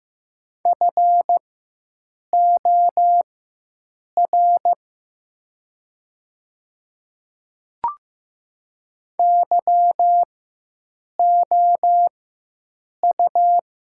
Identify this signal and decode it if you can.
Morse code, heard as a keyed tone, about 15 wpm, spaced as for 7 wpm, 700 Hz. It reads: FOR YOU
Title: 4x Character Spacing - Top 100 Words - 15wpm